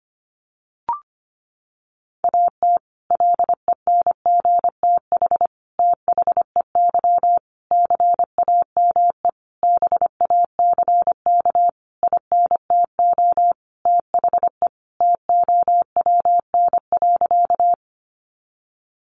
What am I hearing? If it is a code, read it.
Morse code, heard as a keyed tone, about 25 wpm, 700 Hz. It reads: AT LENGTH THEY CAME BACK INTO THE TOWN.
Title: Sentences from Top 1000 Words 25wpm